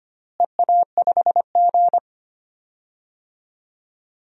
Morse code, heard as a keyed tone, about 25 wpm, 700 Hz. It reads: EA5G
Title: DXCC-100 Call Signs - 4 Characters - Repeated 2x - 25wpm